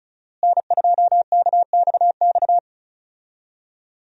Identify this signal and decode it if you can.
Morse code, heard as a keyed tone, about 36 wpm, 700 Hz. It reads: N2KXX
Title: USA Call Signs - 5 Characters - Repeated 3x - 35wpm